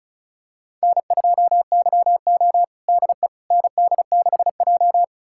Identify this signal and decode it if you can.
Morse code, heard as a keyed tone, about 34 wpm, 700 Hz. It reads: N2YO DE ND6J
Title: USA Callsign DE Callsign - 4 Characters - 35wpm